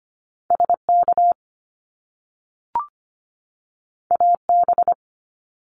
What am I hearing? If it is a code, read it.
Morse code, heard as a keyed tone, about 25 wpm, 700 Hz. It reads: SK AB
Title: Canadian Provinces and Territories 25wpm